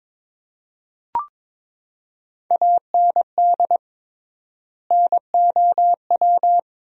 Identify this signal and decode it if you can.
Morse code, heard as a keyed tone, about 22 wpm, 700 Hz. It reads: AND NOW
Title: 3x Word-Spacing - Sentences from Top 300 Words - 22wpm